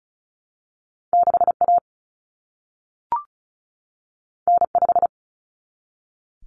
Morse code, heard as a keyed tone, about 36 wpm, 700 Hz. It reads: 6A N5
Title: Common Cut Numbers - 2-Digit 35wpm